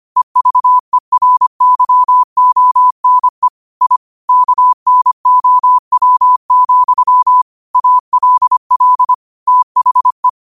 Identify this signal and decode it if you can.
Morse code, heard as a keyed tone, about 25 wpm, 1000 Hz. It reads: EVERYONE I KNOW, ALL THE